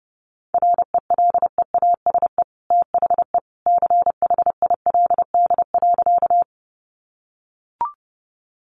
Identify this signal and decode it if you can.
Morse code, heard as a keyed tone, about 30 wpm, 700 Hz. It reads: RELEASE THE CHILD.
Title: Sentences from Top 1100 Words 30wpm